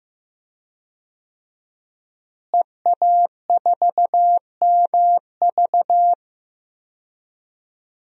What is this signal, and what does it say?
Morse code, heard as a keyed tone, about 15 wpm, 700 Hz. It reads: EA4MV
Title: DXCC-100 Call Signs - 5 Characters - Repeated 3x - 15wpm